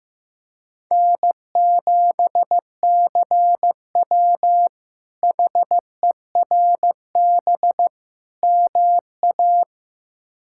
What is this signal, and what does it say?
Morse code, heard as a keyed tone, about 15 wpm, 700 Hz. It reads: N7CW HERB MA
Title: K1USN SST QSO Practice - 15wpm